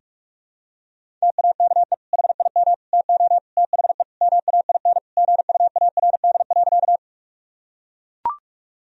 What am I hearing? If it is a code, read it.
Morse code, heard as a keyed tone, about 41 wpm, 700 Hz. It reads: TAKE HIM TO THE MAIN GUARD.
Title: Sentences from Top 1000 Words - Review Part 4 - 45wpm